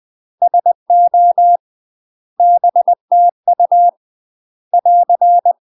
Morse code, heard as a keyed tone, about 20 wpm, 700 Hz. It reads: SO BTU <AR>
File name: UK MASTER QSOs TRAINING / 20WPM / UK MASTER QSOs 0028